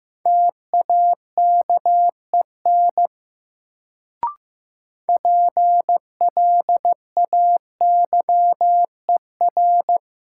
Morse code, heard as a keyed tone, about 15 wpm, 700 Hz. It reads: TAKEN PLAYER
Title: Sentences from Top 1000 Words - Review Part 4 - New Word Drill - 15wpm